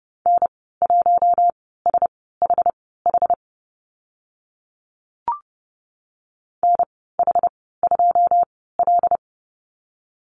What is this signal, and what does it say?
Morse code, heard as a keyed tone, about 30 wpm, spaced as for 17 wpm, 700 Hz. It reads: N1SHH NH2L
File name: Lesson 14g - Number 2 - Call Signs 30wpm